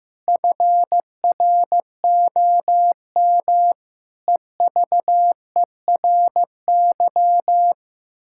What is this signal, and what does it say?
Morse code, heard as a keyed tone, about 15 wpm, 700 Hz. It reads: FROM EVERY
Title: Pangram Challenge v2 - 15wpm